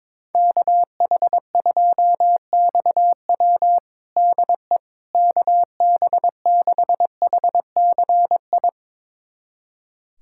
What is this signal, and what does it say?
Morse code, heard as a keyed tone, about 22 wpm, 700 Hz. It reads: KH2XW DE KB6HCI
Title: USA Callsign DE Callsign 22wpm